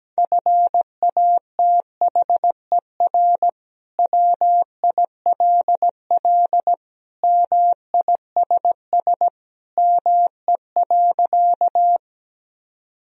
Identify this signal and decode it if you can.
Morse code, heard as a keyed tone, about 17 wpm, 700 Hz. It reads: FATHER WILL MISS ME.
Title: Sentences from Top 1000 Words - Review Part 3 - 17wpm